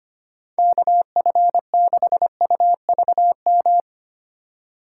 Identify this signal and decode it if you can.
Morse code, heard as a keyed tone, about 25 wpm, 700 Hz. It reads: KF6UVM
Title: USA Call Signs - 6 Characters - 2x Character Spacing - 25wpm